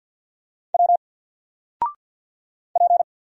Morse code, heard as a keyed tone, about 50 wpm, 700 Hz. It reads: W P
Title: Single Letter-Number 50wpm